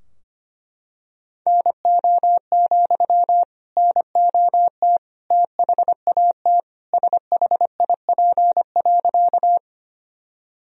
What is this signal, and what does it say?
Morse code, heard as a keyed tone, about 25 wpm, 700 Hz. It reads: NO, NOT THAT SHIP.
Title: Sentences from Top 1000 Words - Review Part 2 - 25wpm